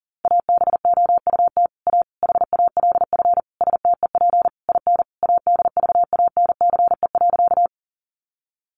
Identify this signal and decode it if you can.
Morse code, heard as a keyed tone, about 40 wpm, 700 Hz. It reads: ABOUT A HALF STEP IN ADVANCE.
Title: Sentences from Top 1400 Words 40wpm